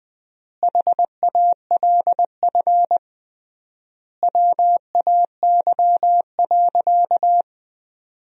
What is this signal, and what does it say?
Morse code, heard as a keyed tone, about 20 wpm, 700 Hz. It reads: HALF WAY.
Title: 3x Word-Spacing - Sentences from Top 400 Words - 20wpm